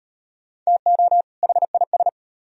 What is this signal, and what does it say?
Morse code, heard as a keyed tone, about 37 wpm, 700 Hz. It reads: TO HIS